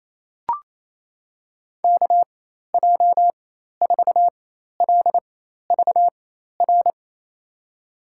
Morse code, heard as a keyed tone, about 27 wpm, spaced as for 13 wpm, 700 Hz. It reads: KJ4LVR
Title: USA Call Signs - 6 Characters - 4x Character Spacing - 28wpm